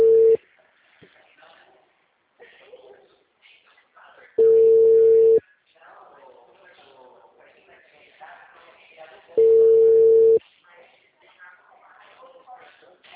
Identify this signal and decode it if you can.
Ringback tone (China, 450 Hz), with speech.
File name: out- -2007- - - .20860